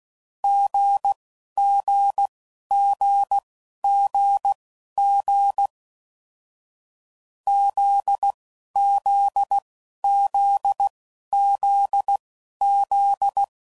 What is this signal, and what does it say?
Morse code, heard as a keyed tone, about 16 wpm, spaced as for 11 wpm, 800 Hz. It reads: GGGGG ZZZZZ
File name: Combinations / Difficult Combos / GZ GZ7